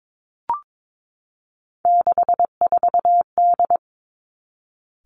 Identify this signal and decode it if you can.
Morse code, heard as a keyed tone, about 22 wpm, 700 Hz. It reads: 64D
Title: Three Challenging Letter-Number Combinations 22wpm